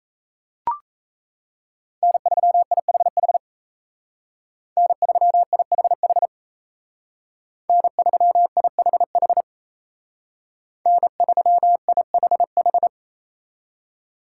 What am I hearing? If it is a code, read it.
Morse code, heard as a keyed tone, about 35 wpm, 700 Hz. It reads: N3IHH N3IHH N3IHH N3IHH